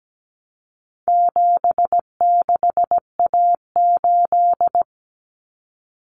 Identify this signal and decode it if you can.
Morse code, heard as a keyed tone, about 17 wpm, 700 Hz. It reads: 76A8
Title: Common Cut Numbers - 4-Digit 17wpm